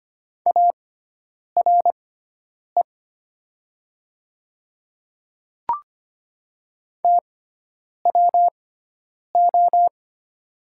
Morse code, heard as a keyed tone, about 25 wpm, spaced as for 9 wpm, 700 Hz. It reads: ARE TWO
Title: 6x Character Spacing - Top 100 Words - 25wpm